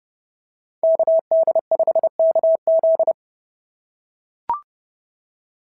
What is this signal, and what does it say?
Morse code, heard as a keyed tone, about 30 wpm, 690 Hz. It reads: KD5KZ
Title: SOTA QSO Practice 30wpm